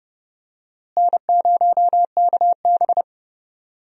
Morse code, heard as a keyed tone, about 30 wpm, 700 Hz. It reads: N0KB
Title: USA Call Signs - 4 Characters - Repeated 3x - 30wpm